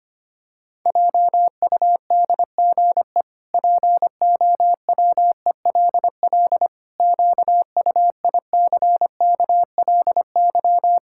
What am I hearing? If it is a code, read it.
Morse code, heard as a keyed tone, about 25 wpm, 700 Hz. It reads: JUDGE POWELL QUICKLY
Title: Pangram Challenge v2 - 25wpm